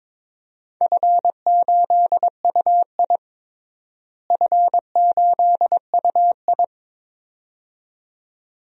Morse code, heard as a keyed tone, about 22 wpm, 700 Hz. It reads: F8UI F8UI
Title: DXCC-100 Call Signs - 4 Characters - Repeated 2x - 22wpm